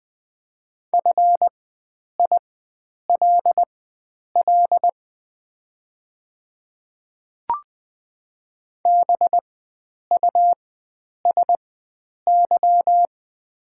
Morse code, heard as a keyed tone, about 21 wpm, spaced as for 9 wpm, 700 Hz. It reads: FILL BUSY
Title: 4x Character Spacing - Top 500 Words - 20wpm